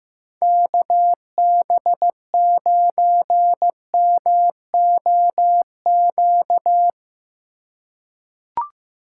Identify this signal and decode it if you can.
Morse code, heard as a keyed tone, about 15 wpm, 700 Hz. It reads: KB9MOQ
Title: USA Call Signs - 6 Characters - 2x Character Spacing - 15wpm